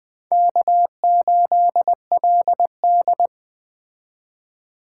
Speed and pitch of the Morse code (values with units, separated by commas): 21 wpm, 700 Hz